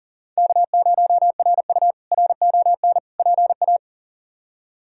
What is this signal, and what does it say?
Morse code, heard as a keyed tone, about 40 wpm, 700 Hz. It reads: K0RU RON PA
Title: K1USN SST QSO Practice - 40wpm